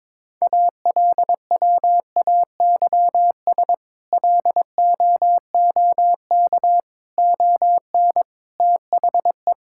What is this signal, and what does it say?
Morse code, heard as a keyed tone, about 22 wpm, 700 Hz. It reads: ALWAYS LOOK ON THE
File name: practice / 22wpm / Sentences with 8-Word Phrases - 22wpm